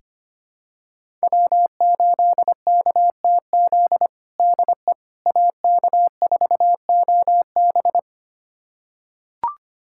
Morse code, heard as a keyed tone, about 25 wpm, 700 Hz. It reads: W8KTZ DE AK4OB
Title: USA Callsign DE Callsign - 5 Characters - 25wpm